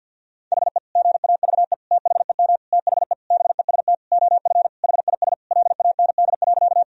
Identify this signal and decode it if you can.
Morse code, heard as a keyed tone, about 51 wpm, 700 Hz. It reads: HE GAVE THEM THE BEST OF HIS LAND.
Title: Sentences from Top 1000 Words - Review Part 1 - 50wpm